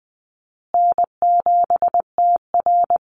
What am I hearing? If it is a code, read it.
Morse code, heard as a keyed tone, about 20 wpm, 700 Hz. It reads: N7TR
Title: Call Signs Repeated 2x - 20wpm